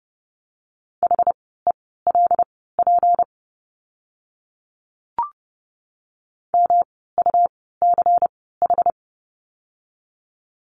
Morse code, heard as a keyed tone, about 30 wpm, spaced as for 17 wpm, 700 Hz. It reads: HELP MUCH
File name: course / 30:10wpm / 237 Lesson 38f - Letter q - Words 30wpm